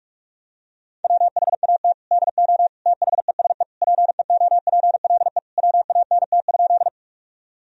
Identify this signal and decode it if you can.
Morse code, heard as a keyed tone, about 42 wpm, 700 Hz. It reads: WHAT DO THESE PEOPLE WANT?
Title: Sentences from Top 1000 Words - Review Part 1 - 45wpm